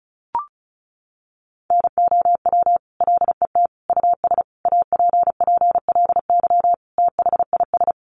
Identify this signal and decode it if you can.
Morse code, heard as a keyed tone, about 34 wpm, 700 Hz. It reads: NOW LET US APPLY THIS
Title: Sentences from Top 1200 Words 35wpm